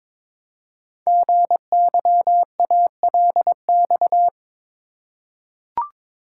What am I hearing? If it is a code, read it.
Morse code, heard as a keyed tone, about 22 wpm, 700 Hz. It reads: GYALX
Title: Code Groups - 5 Letters - 22wpm